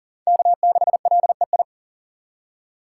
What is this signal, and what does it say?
Morse code, heard as a keyed tone, about 40 wpm, 700 Hz. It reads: K6LEI
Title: USA Call Signs - 2x Character Spacing - 40wpm